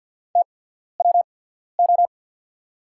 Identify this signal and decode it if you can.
Morse code, heard as a keyed tone, about 50 wpm, spaced as for 14 wpm, 690 Hz. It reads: TWO
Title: 8x Character Spacing - Top 100 Words - 50wpm